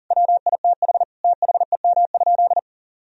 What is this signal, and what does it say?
Morse code, heard as a keyed tone, about 40 wpm, 700 Hz. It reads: WITH THEM?
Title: Sentences from Top 1000 Words - Review Part 1 - 40wpm